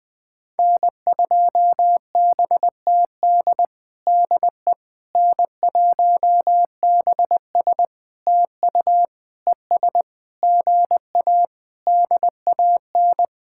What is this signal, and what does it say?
Morse code, heard as a keyed tone, about 20 wpm, 700 Hz. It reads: N2BTD DE N1BS TU ES GA DAN
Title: POTA QSO Practice - 20wpm